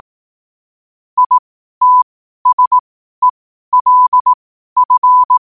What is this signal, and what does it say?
Morse code, heard as a keyed tone, about 18 wpm, spaced as for 12 wpm, 1000 Hz. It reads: ITSELF